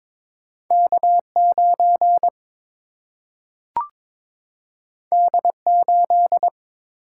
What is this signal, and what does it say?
Morse code, heard as a keyed tone, about 22 wpm, 700 Hz. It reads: K 9 D 8
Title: Single-Letter Single-Number 22wpm